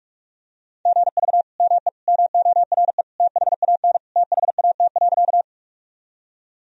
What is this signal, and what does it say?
Morse code, heard as a keyed tone, about 44 wpm, 700 Hz. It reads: GV ME MORE THAN THAT.
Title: Sentences from Top 200 Words and Abbreviations 45wpm